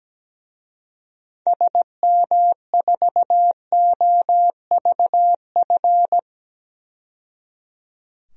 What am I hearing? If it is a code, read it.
Morse code, heard as a keyed tone, about 17 wpm, 700 Hz. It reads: SM4OVF
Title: DXCC-100 Call Signs - 6 Characters - 17wpm